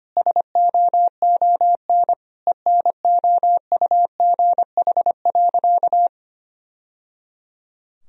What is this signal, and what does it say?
Morse code, heard as a keyed tone, about 25 wpm, 700 Hz. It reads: SOON ENOUGH.
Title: Sentences from Top 1000 Words - Review Part 4 - 25wpm